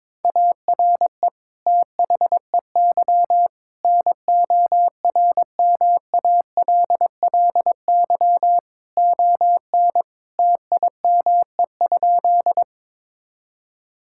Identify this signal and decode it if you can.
Morse code, heard as a keyed tone, about 22 wpm, 700 Hz. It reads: ARE THEY NORMALLY ON TIME?
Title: Sentences from Top 1000 Words - Review Part 6 - 22wpm